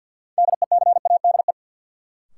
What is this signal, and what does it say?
Morse code, heard as a keyed tone, about 51 wpm, 700 Hz. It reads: DECADE
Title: Sets of 5 Words - New Word Drill - 50wpm